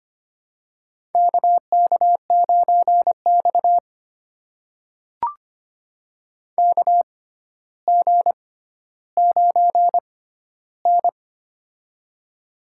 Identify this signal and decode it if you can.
Morse code, heard as a keyed tone, about 25 wpm, spaced as for 13 wpm, 700 Hz. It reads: KK9X KG9N
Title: USA Call Signs - 4 Characters - 6x Character Spacing - 25wpm